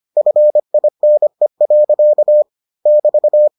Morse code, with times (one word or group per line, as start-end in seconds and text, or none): 0.2-2.4 FINE.
2.9-3.5 <BT>